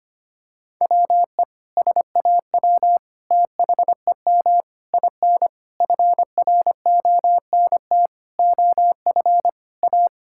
Morse code, heard as a keyed tone, about 25 wpm, 700 Hz. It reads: WE SAW THEM IN FRONT OF A
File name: practice / 25wpm / Sentences with 4-Word Phrases - 25wpm